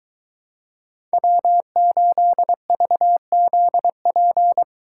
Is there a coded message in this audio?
Yes: Morse code from 1.1 to 4.6 s, heard as a keyed tone, about 22 wpm, 700 Hz. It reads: W8VZP